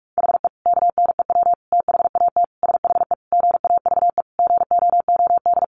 Morse code, heard as a keyed tone, about 44 wpm, 700 Hz. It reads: HE KNEW THAT SHE GAVE GOOD
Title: Sentences from Top 1100 Words 45wpm